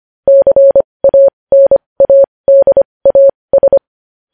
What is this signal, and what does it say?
Morse code, heard as a keyed tone, about 25 wpm, spaced as for 20 wpm, 550 Hz. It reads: CANADAS
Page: News Headlines in Morse Code at 20 WPM